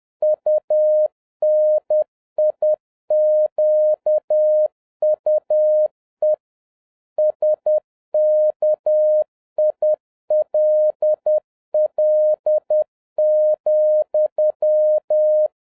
Morse code, heard as a keyed tone, about 10 wpm, 610 Hz. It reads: UNIQUE SKILL,
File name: 2011-07-mp3 / n21 GREETINGS FROM SINGAPORE 10wpm